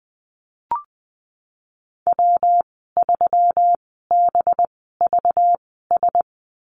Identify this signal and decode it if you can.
Morse code, heard as a keyed tone, about 20 wpm, spaced as for 15 wpm, 700 Hz. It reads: W3BVS
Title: 193 Lesson 31g - Letter v - Call Signs 20wpm